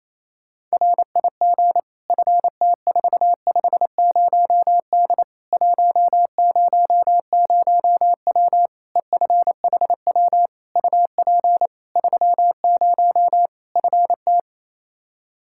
Morse code, heard as a keyed tone, about 28 wpm, 700 Hz. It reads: RIG FT450D 100W EFHW UP 30 FT
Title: K1USN SST QSO Practice 28wpm